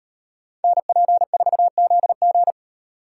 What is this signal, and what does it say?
Morse code, heard as a keyed tone, about 38 wpm, 700 Hz. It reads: NP4ZG